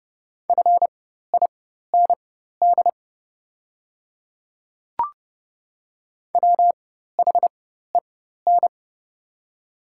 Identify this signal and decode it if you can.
Morse code, heard as a keyed tone, about 30 wpm, spaced as for 14 wpm, 700 Hz. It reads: FIND WHEN
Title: 4x Character Spacing - Top 100 Words - 30wpm